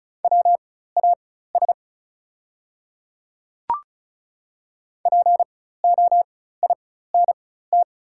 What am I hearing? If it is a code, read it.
Morse code, heard as a keyed tone, about 36 wpm, spaced as for 16 wpm, 690 Hz. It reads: WAS POINT
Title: 4x Character Spacing - Top 200 Words - 35wpm